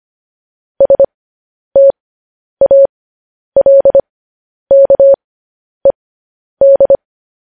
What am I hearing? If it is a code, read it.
Morse code, heard as a keyed tone, about 25 wpm, spaced as for 10 wpm, 550 Hz. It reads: STALKED
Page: news-headlines-in-morse-code-at-10-wpm